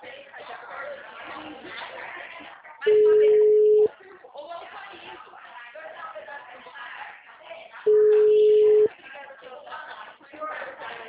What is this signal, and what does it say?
Ringback tone (Europe, 425 Hz), with speech.
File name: out- -2010- - - .10468